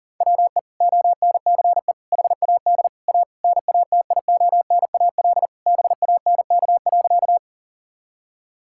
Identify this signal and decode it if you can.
Morse code, heard as a keyed tone, about 40 wpm, 700 Hz. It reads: WE ONCE HAD A NATIONAL BANK.
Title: Sentences from Top 1000 Words - Review Part 6 - 40wpm